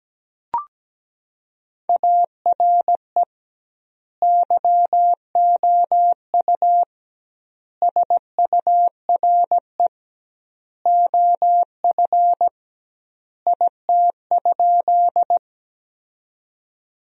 Morse code, heard as a keyed tone, about 17 wpm, 700 Hz. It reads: ARE YOU SURE OF IT?
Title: 2x Word-Spacing - Sentences from Top 300 Words - 17wpm